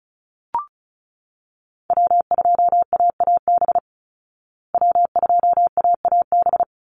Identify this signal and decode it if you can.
Morse code, heard as a keyed tone, about 34 wpm, 700 Hz. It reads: W2AAB W2AAB
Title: Call Signs Repeated 2x - 35wpm